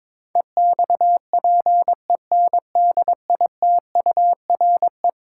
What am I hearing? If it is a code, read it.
Morse code, heard as a keyed tone, about 22 wpm, 700 Hz. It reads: EXPENDITURE
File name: practice / 22wpm / Eleven Letter Words - 22wpm